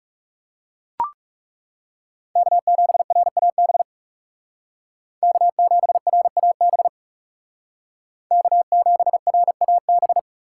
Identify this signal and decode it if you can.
Morse code, heard as a keyed tone, about 36 wpm, 700 Hz. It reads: K7RAB K7RAB K7RAB
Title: Speed Racing - USA 1x3 Call Signs 30wpm